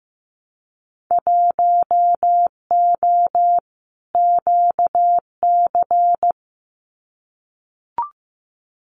Morse code, heard as a keyed tone, about 15 wpm, 700 Hz. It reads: 1O QC
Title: Winter Field Day Exchanges 15wpm